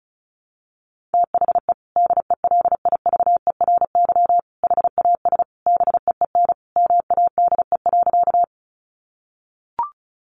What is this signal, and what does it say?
Morse code, heard as a keyed tone, about 34 wpm, 700 Hz. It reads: THE DELIVERY HAS BEEN MADE.
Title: Sentences from Top 1500 Words 35wpm